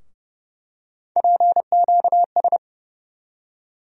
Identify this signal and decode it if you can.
Morse code, heard as a keyed tone, about 31 wpm, 700 Hz. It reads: PQS
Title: USA Call Sign Postfixes - 3 Letters 30wpm